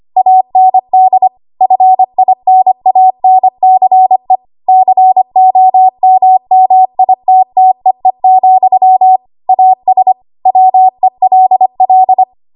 Morse code, heard as a keyed tone, about 25 wpm, 750 Hz. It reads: AND FINANCE COMMITTEE, AS WELL
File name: Archive / 25 WPM / 25WPM